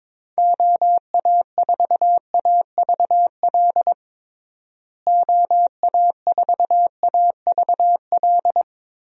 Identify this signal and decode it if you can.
Morse code, heard as a keyed tone, about 22 wpm, 700 Hz. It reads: OA4AVL OA4AVL
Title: USA Call Signs - 6 Characters - Repeated 3x - 22wpm